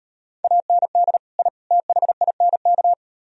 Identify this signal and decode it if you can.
Morse code, heard as a keyed tone, about 37 wpm, 700 Hz. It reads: AND I THINK